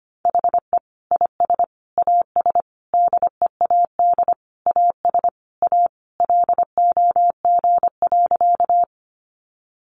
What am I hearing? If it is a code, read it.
Morse code, heard as a keyed tone, about 25 wpm, 700 Hz. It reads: HE IS AS DEAD AS A LOG.
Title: Sentences from Top 1500 Words 25wpm